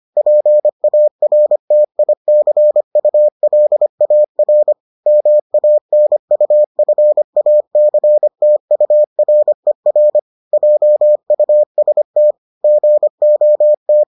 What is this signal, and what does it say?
Morse code, heard as a keyed tone, about 25 wpm, 600 Hz. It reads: PARTICULAR MANUFACTURER JUST GOT